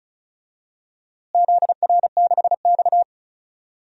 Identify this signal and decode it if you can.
Morse code, heard as a keyed tone, about 36 wpm, 700 Hz. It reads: ZR6X
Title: DXCC-100 Call Signs - 4 Characters - Repeated 3x - 35wpm